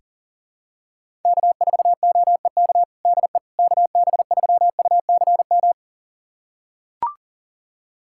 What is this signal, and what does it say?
Morse code, heard as a keyed tone, about 37 wpm, 700 Hz. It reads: K4OEK DE KB3UCM
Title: USA Callsign DE Callsign 40wpm